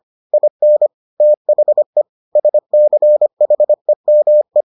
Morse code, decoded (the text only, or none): IN THE SCHEME